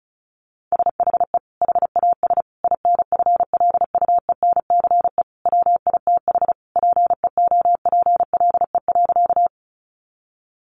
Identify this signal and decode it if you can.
Morse code, heard as a keyed tone, about 34 wpm, 700 Hz. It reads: SHE HAS INFLUENCE WITH PEOPLE.
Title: Sentences from Top 1100 Words 35wpm